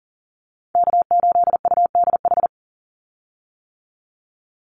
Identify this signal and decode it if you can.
Morse code, heard as a keyed tone, about 38 wpm, 700 Hz. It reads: K8UDH
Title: Call Signs - 40wpm